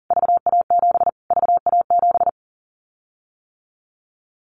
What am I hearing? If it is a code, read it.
Morse code, heard as a keyed tone, about 40 wpm, 700 Hz. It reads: VA7 VA7
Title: Callsign Prefixes for Top 25 DXCC - Repeated 3x - 40wpm